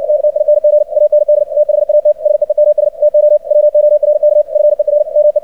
Morse code, heard as a keyed tone, about 60 wpm, 600 Hz, with radio noise. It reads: TTING AND THAT SEEMS TO WORK FER ME
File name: archived / CRSnet